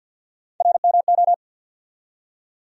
Morse code, heard as a keyed tone, about 50 wpm, 700 Hz. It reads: RMO
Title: USA Call Sign Postfixes - 3 Letters 50wpm